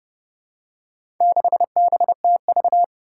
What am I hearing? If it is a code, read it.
Morse code, heard as a keyed tone, about 30 wpm, 700 Hz. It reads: 6BTV
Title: Four Challenging Letter-Number Combinations 30wpm